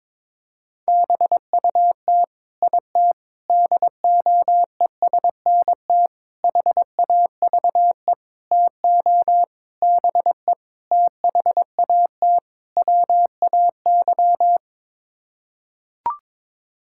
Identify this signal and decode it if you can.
Morse code, heard as a keyed tone, about 22 wpm, 700 Hz. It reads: BUT IT DOESNT HAVE TO BE THAT WAY
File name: practice / 22wpm / Sets of 8 Words - 22wpm